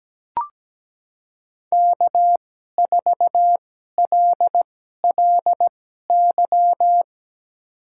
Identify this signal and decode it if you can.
Morse code, heard as a keyed tone, about 17 wpm, spaced as for 12 wpm, 700 Hz. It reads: K4LLY
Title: USA Call Signs - 5 Characters - 2x Character Spacing - 17wpm